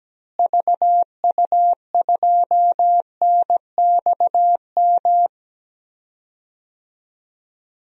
Morse code, heard as a keyed tone, about 17 wpm, 700 Hz. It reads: VU2NXM
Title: DXCC-100 Call Signs - 6 Characters - Repeated 3x - 17wpm